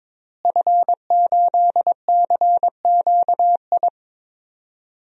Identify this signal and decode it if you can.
Morse code, heard as a keyed tone, about 22 wpm, 700 Hz. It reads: F8CQI
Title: DXCC-100 Call Signs - 5 Characters - Repeated 3x - 22wpm